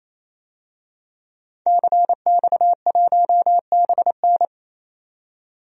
Morse code, heard as a keyed tone, about 28 wpm, 700 Hz. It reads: CX1BN
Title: DXCC-100 Call Signs - 5 Characters - 28wpm